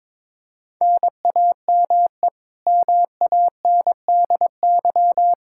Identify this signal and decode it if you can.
Morse code, heard as a keyed tone, about 22 wpm, 700 Hz. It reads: NAME MANDY
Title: Practice QSO - Exchange 1 - Beginning - 22wpm